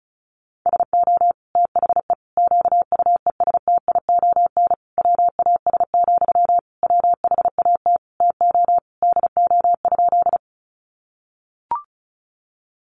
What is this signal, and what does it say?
Morse code, heard as a keyed tone, about 34 wpm, 700 Hz. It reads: SO THE QUESTION WAS, WHAT TO DO?
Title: Sentences from Top 400 Words 35wpm